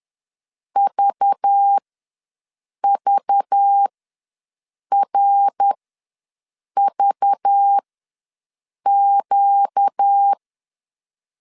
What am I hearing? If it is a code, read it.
Morse code, heard as a keyed tone, about 10 wpm, spaced as for 6 wpm, 800 Hz. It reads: VVRVQ
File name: LEZIONI / 5 LBVC / 5 LBV ws 10wpm 3645